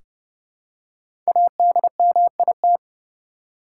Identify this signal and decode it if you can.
Morse code, heard as a keyed tone, about 30 wpm, 700 Hz. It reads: ADMIT
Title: Five Letter Words - 30wpm